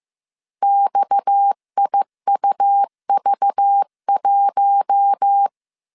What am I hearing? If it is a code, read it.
Morse code, heard as a keyed tone, about 15 wpm, 800 Hz. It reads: XIUV1
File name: CW / DET / 1 6 1500C S C 14wpm 2951